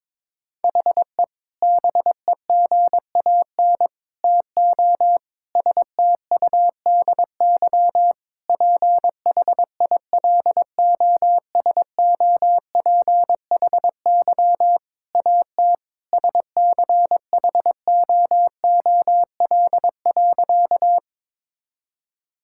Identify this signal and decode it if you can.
Morse code, heard as a keyed tone, about 21 wpm, 700 Hz. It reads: HE BEGAN TO STUDY PHILOSOPHY AT SCHOOL.
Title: Sentences from Top 1000 Words - Review Part 6 - 22wpm